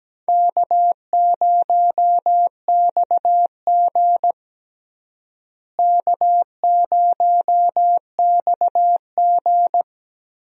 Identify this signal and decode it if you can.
Morse code, heard as a keyed tone, about 17 wpm, 700 Hz. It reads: K0XG K0XG
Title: USA Call Signs - 4 Characters - Repeated 3x - 17wpm